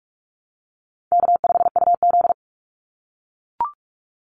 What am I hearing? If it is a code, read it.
Morse code, heard as a keyed tone, about 44 wpm, 700 Hz. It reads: K5UZ
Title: Easy Call Signs - 45wpm